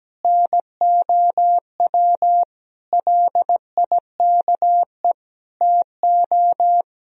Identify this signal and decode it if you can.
Morse code, heard as a keyed tone, about 17 wpm, 700 Hz. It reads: NOW LIKE TO
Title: Sentences from Top 200 Words and Abbreviations 17wpm